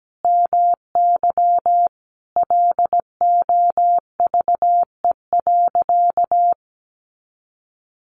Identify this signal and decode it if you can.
Morse code, heard as a keyed tone, about 17 wpm, 700 Hz. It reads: MY LOVE.
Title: Sentences from Top 400 Words 17wpm